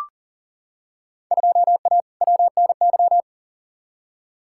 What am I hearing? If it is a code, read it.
Morse code, heard as a keyed tone, about 40 wpm, 700 Hz. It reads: 2A WNY
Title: Field Day Exchanges - 40wpm